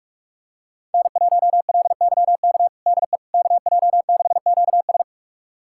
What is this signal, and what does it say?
Morse code, heard as a keyed tone, about 46 wpm, 700 Hz. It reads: N1LYK DE KJ6QS
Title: USA Callsign DE Callsign - 5 Characters - 45wpm